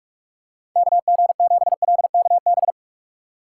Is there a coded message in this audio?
Yes: Morse code from 0.8 to 2.7 s, heard as a keyed tone, about 44 wpm, 700 Hz. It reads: KG7LKB